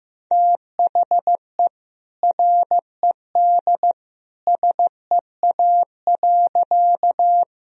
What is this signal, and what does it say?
Morse code, heard as a keyed tone, about 15 wpm, 700 Hz. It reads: THE RED SEA.
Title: Sentences from Top 1000 Words - Review Part 2 - 15wpm